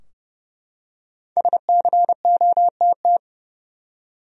Morse code, heard as a keyed tone, about 29 wpm, 700 Hz. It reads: SCOTT
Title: USA Common Names - 30wpm